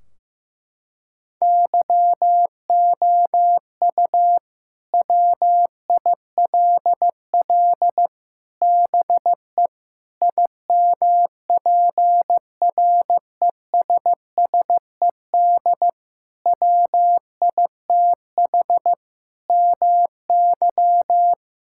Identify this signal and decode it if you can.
Morse code, heard as a keyed tone, about 15 wpm, 700 Hz. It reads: YOU WILL BE IMPRESSED WITH MY
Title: Sentences from Top 1600 Words 15wpm